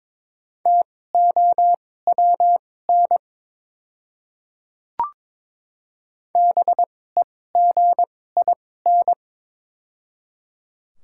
Morse code, heard as a keyed tone, about 23 wpm, spaced as for 16 wpm, 700 Hz. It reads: TOWN BEGIN
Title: 2x Character Spacing - Top 500 Words - 22wpm